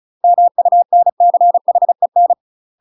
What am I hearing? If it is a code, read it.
Morse code, heard as a keyed tone, about 37 wpm, 700 Hz. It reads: MUNCHEN
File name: UK MASTER QSOs TRAINING / 35WPM / UK MASTER QSOs 0008